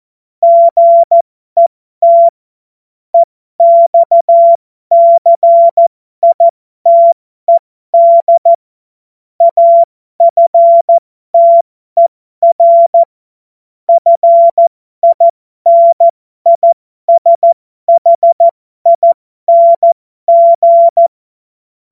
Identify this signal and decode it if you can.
Morse code, heard as a keyed tone, about 14 wpm, spaced as for 12 wpm, 680 Hz. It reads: GET EXCITED AFTER FINISHING
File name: Morsekurs / Witze